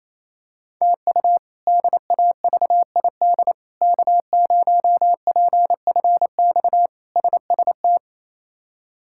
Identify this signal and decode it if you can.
Morse code, heard as a keyed tone, about 28 wpm, 700 Hz. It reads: TU DAVID K0PFX SST
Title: K1USN SST QSO Practice - 28wpm